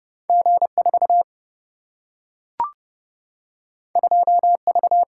Morse code, heard as a keyed tone, about 30 wpm, 700 Hz. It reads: G 4 2 V
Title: Single-Letter Single-Number 30wpm